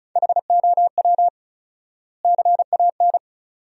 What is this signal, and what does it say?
Morse code, heard as a keyed tone, about 34 wpm, 700 Hz. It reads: HOW CAN
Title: 4x Word-Spacing - Sentences from Top 200 Words - 35wpm